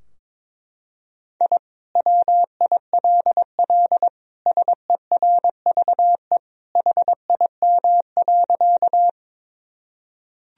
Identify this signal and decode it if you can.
Morse code, heard as a keyed tone, about 21 wpm, 700 Hz. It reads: I WILL SERVE HIM.
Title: Sentences from Top 1000 Words - Review Part 2 - 22wpm